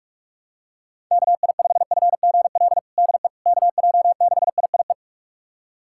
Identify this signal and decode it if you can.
Morse code, heard as a keyed tone, about 46 wpm, 700 Hz. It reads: KI5FGL DE KJ6IIE
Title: USA Callsign DE Callsign - 6 Characters - 45wpm